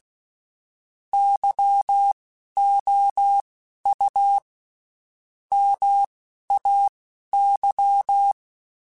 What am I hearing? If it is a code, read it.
Morse code, heard as a keyed tone, about 16 wpm, spaced as for 11 wpm, 800 Hz. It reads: YOU MAY
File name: Spaced / Chuckles Spaced-mp3s / SoftWords sp @15